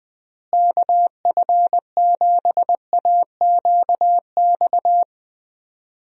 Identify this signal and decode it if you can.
Morse code, heard as a keyed tone, about 20 wpm, 700 Hz. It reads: KF7AQX
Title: USA Call Signs - 6 Characters - Repeated 3x - 20wpm